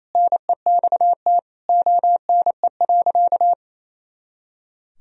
Morse code, heard as a keyed tone, about 28 wpm, 700 Hz. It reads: NEXT ONE.
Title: Sentences from Top 400 Words and Abbreviations 28wpm